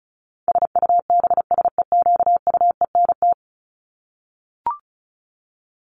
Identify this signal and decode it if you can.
Morse code, heard as a keyed tone, about 34 wpm, 700 Hz. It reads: SUBSEQUENT
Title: LY Suffix - Top 100 Words with Base Words 35wpm